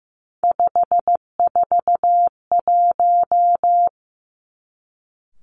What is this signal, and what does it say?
Morse code, heard as a keyed tone, about 15 wpm, 700 Hz. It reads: 541
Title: Three Challenging Letter-Number Combinations 15wpm